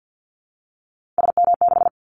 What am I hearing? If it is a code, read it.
Morse code, heard as a keyed tone, about 49 wpm, 700 Hz. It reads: SM6
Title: Callsign Prefixes for Top 25 DXCC - Repeated 3x - 50wpm